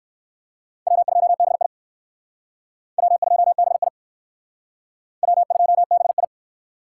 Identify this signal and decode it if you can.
Morse code, heard as a keyed tone, about 59 wpm, 700 Hz. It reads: W2BI W2BI W2BI